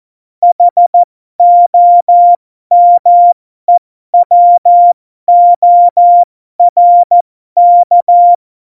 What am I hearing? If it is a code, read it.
Morse code, heard as a keyed tone, about 14 wpm, spaced as for 12 wpm, 700 Hz. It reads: HOMEWORK
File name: Morsekurs / Witze